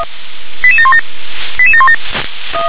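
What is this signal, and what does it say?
Five-tone selective call (ZVEI), heard as quick runs of five tones, tones of about 70 ms. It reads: 88717 88717
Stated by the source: ZVEI five-tone selective call received by SDR.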